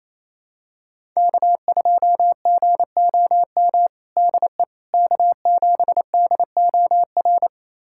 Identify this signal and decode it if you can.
Morse code, heard as a keyed tone, about 28 wpm, 700 Hz. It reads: K2GOM DE K7DOR
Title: USA Callsign DE Callsign - 5 Characters - 28wpm